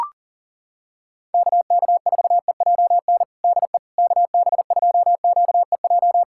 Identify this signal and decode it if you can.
Morse code, heard as a keyed tone, about 37 wpm, 700 Hz. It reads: KK4EJN DE KB2QEJ
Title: USA Callsign DE Callsign 40wpm